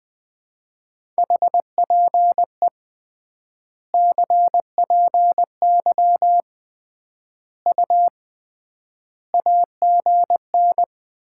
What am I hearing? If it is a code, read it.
Morse code, heard as a keyed tone, about 20 wpm, 700 Hz. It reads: HPE CPY U AGN
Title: Common QSO Phrases - 3x Word-Spacing - 20wpm